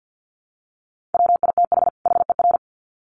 Morse code, heard as a keyed tone, about 50 wpm, 700 Hz. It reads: WITH HER
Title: Sets of 2 Words 50wpm